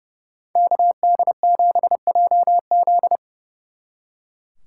Morse code, heard as a keyed tone, about 36 wpm, 700 Hz. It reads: KD7JZ
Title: Speed Racing - USA Call Signs 30wpm